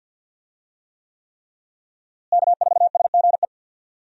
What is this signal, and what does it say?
Morse code, heard as a keyed tone, about 52 wpm, 700 Hz. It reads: K4SGE
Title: USA Call Signs - 5 Characters - Repeated 3x - 50wpm